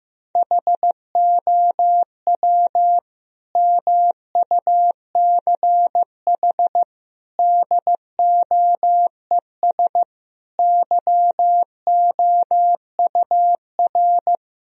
Morse code, heard as a keyed tone, about 16 wpm, 700 Hz. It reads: HOW MUCH DOES YOUR
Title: Sentences from Top 1000 Words - Review Part 4 - 15wpm